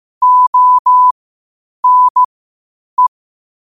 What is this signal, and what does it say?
Morse code, heard as a keyed tone, about 15 wpm, spaced as for 8 wpm, 1000 Hz. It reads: ONE
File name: qotdfiles / 8wpm